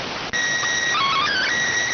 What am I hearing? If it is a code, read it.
SSTV (slow-scan television), PD 290.